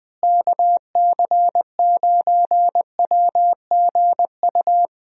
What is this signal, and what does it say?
Morse code, heard as a keyed tone, about 20 wpm, 700 Hz. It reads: KC9WGU
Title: USA Call Signs - 6 Characters - Repeated 3x - 20wpm